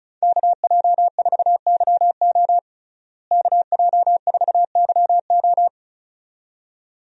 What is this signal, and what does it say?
Morse code, heard as a keyed tone, about 34 wpm, 700 Hz. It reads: KJ4YO KJ4YO
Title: USA Call Signs - 5 Characters - Repeated 2x - 35wpm